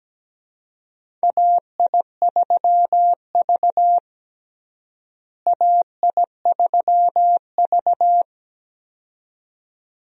Morse code, heard as a keyed tone, about 17 wpm, 700 Hz. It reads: AI3V AI3V
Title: USA Call Signs - 4 Characters - Repeated 2x - 17wpm